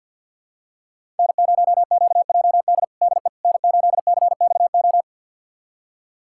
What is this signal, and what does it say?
Morse code, heard as a keyed tone, about 53 wpm, 700 Hz. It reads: N0QJD DE N8CXO
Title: USA Callsign DE Callsign - 5 Characters - 50wpm